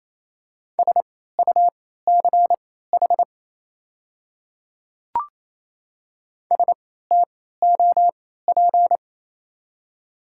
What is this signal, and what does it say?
Morse code, heard as a keyed tone, about 28 wpm, spaced as for 16 wpm, 700 Hz. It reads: SUCH STOP
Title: 3x Character Spacing - Top 300 Words - 28wpm